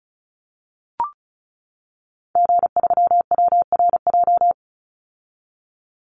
Morse code, heard as a keyed tone, about 34 wpm, 700 Hz. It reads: G3WRJ
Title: Call Signs - 35wpm